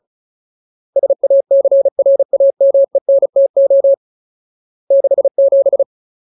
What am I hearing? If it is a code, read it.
Morse code, heard as a keyed tone, about 36 wpm, 550 Hz. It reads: SACRAMENTO 67
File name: Ultimate Word List / 35WPM Ultimate